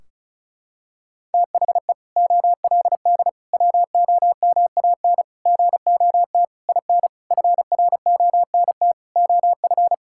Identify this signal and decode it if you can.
Morse code, heard as a keyed tone, about 36 wpm, 700 Hz. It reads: THE OLD WOMAN GOT IN FRONT OF
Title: Sentences from Top 1600 Words 35wpm